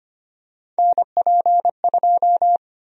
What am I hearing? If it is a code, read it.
Morse code, heard as a keyed tone, about 25 wpm, 700 Hz. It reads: NP2
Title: USA Call Sign Prefixes - Repeated 2x - 25wpm